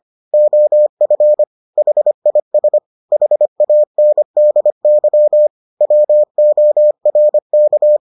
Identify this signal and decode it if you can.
Morse code, heard as a keyed tone, about 25 wpm, 600 Hz. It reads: OF HIS HANDY WORK